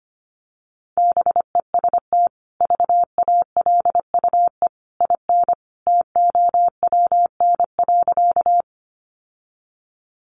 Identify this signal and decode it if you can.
Morse code, heard as a keyed tone, about 25 wpm, 700 Hz. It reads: BEST VALUE IN TOWN.
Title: Sentences from Top 600 Words 25wpm